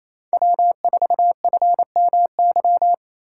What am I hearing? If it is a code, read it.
Morse code, heard as a keyed tone, about 37 wpm, 700 Hz. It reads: W4FMY